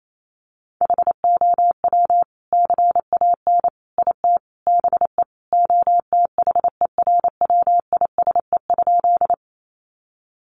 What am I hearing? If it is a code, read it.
Morse code, heard as a keyed tone, about 28 wpm, 700 Hz. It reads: HOW CAN IT BE OTHERWISE?
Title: Sentences from Top 1200 Words 28wpm